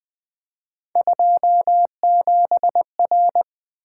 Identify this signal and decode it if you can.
Morse code, heard as a keyed tone, about 25 wpm, 700 Hz. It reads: 27R
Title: Speed-Racing - Three Letter-Number Combinations - 20wpm